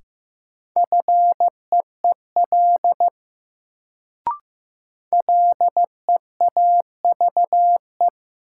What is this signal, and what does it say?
Morse code, heard as a keyed tone, about 15 wpm, 700 Hz. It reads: FEEL LEAVE
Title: Sentences from Top 400 Words - New Word Drill - 15wpm